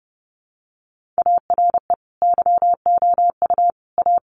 Morse code, heard as a keyed tone, about 28 wpm, 700 Hz. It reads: ARE YOU A
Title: Sentences from Top 1500 Words 30wpm